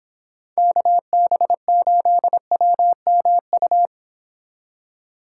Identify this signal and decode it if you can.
Morse code, heard as a keyed tone, about 29 wpm, 700 Hz. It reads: KB8WMU